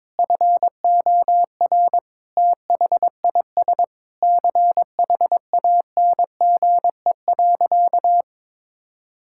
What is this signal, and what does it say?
Morse code, heard as a keyed tone, about 22 wpm, 700 Hz. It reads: FOR THIS CHANGE.
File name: practice / 22wpm / Sentences with 8-Word Phrases - 22wpm